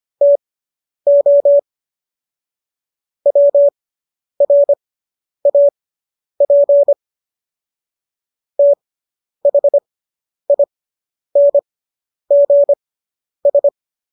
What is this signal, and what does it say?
Morse code, heard as a keyed tone, about 25 wpm, spaced as for 10 wpm, 570 Hz. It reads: TO WRAP THINGS